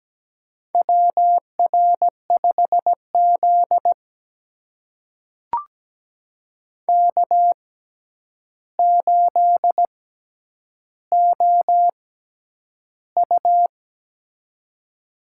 Morse code, heard as a keyed tone, about 17 wpm, spaced as for 9 wpm, 700 Hz. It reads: WR5Z K8OU
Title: USA Call Signs - 4 Characters - 6x Character Spacing - 17wpm